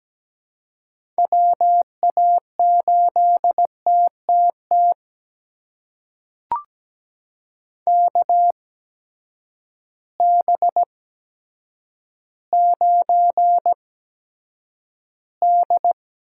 Morse code, heard as a keyed tone, about 17 wpm, spaced as for 8 wpm, 700 Hz. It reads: WA8TTT KB9D
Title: USA Call Signs - 6 Characters - 8x Character Spacing - 17wpm